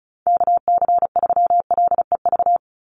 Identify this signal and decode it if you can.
Morse code, heard as a keyed tone, about 34 wpm, 700 Hz. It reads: KC3LEV
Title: USA Call Signs 35wpm